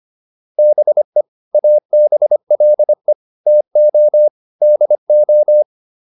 Morse code, heard as a keyed tone, about 25 wpm, 600 Hz. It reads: BE ABLE TO DO